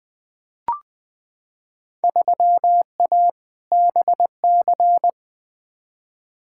Morse code, heard as a keyed tone, about 20 wpm, 700 Hz. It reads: 3A BC
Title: Field Day Exchanges - 20wpm